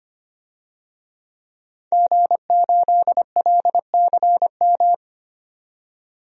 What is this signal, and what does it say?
Morse code, heard as a keyed tone, about 25 wpm, 700 Hz. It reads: G8LCM
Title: DXCC-100 Call Signs - 5 Characters - 25wpm